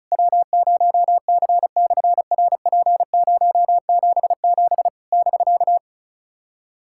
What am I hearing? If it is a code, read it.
Morse code, heard as a keyed tone, about 36 wpm, 700 Hz. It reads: W0C/RP077 <BK>
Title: SOTA QSO Practice 35wpm